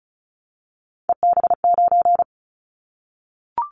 Morse code, heard as a keyed tone, about 36 wpm, 700 Hz. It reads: EB9
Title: Callsign Prefixes for Top 100 DXCC - 35wpm